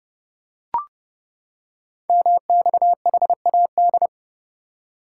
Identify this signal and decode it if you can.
Morse code, heard as a keyed tone, about 30 wpm, 700 Hz. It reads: MXHAD
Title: Code Groups - 5 Letters - 30wpm